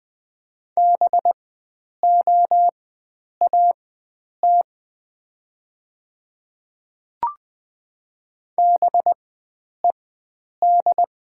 Morse code, heard as a keyed tone, about 20 wpm, spaced as for 9 wpm, 700 Hz. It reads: BOAT BED
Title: 4x Character Spacing - Top 500 Words - 20wpm